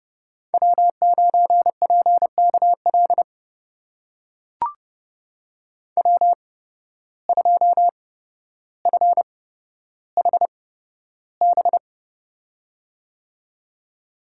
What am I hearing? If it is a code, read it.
Morse code, heard as a keyed tone, about 30 wpm, spaced as for 13 wpm, 700 Hz. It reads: W9PKL W2FHB
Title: USA Call Signs - 5 Characters - 8x Character Spacing - 30wpm